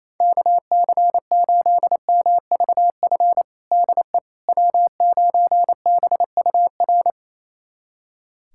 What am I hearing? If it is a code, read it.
Morse code, heard as a keyed tone, about 28 wpm, 700 Hz. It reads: KC8MVF DE W9BUR
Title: USA Callsign DE Callsign 28wpm